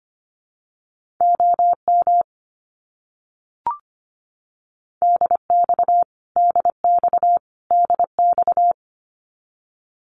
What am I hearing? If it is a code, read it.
Morse code, heard as a keyed tone, about 25 wpm, 700 Hz. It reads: OM DX DX DX
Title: Abbreviations Repeated 3x - 25wpm